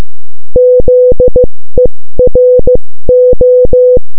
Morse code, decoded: ZERO